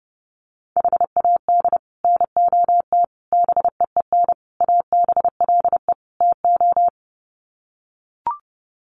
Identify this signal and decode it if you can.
Morse code, heard as a keyed tone, about 30 wpm, 700 Hz. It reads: HAD NOT BEEN ABLE TO
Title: Sets of 5 Words Encore 30wpm